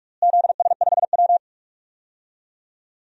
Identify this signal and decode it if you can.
Morse code, heard as a keyed tone, about 41 wpm, 700 Hz. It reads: ZS5W
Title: DXCC-100 Call Signs - 4 Characters - Repeated 2x - 45wpm